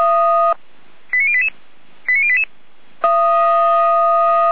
Five-tone selective call (ZVEI), heard as quick runs of five tones, tones of about 70 ms. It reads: 89088 89088